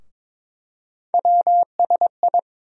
Morse code, heard as a keyed tone, about 22 wpm, 700 Hz. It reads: WSI
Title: USA Call Sign Postfixes - 3 Letters 22wpm